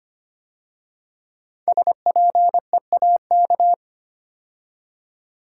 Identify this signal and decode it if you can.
Morse code, heard as a keyed tone, about 25 wpm, 700 Hz. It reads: SPEAK
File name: 5 Letter Words - First 3 Letters Given - 25wpm